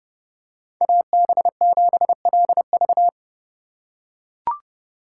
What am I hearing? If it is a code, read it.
Morse code, heard as a keyed tone, about 30 wpm, 700 Hz. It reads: AB7LV
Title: USA Call Signs - 5 Characters - 3x Character Spacing - 30wpm